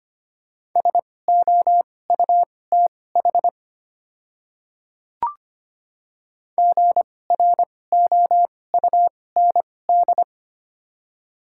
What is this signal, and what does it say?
Morse code, heard as a keyed tone, about 25 wpm, spaced as for 18 wpm, 690 Hz. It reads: SOUTH GROUND
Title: 2x Character Spacing - Top 500 Words - 25wpm